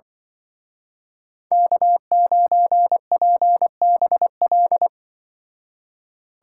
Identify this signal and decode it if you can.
Morse code, heard as a keyed tone, about 24 wpm, 700 Hz. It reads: K9PBL
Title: Speed Racing - USA 1x3 Call Signs 20wpm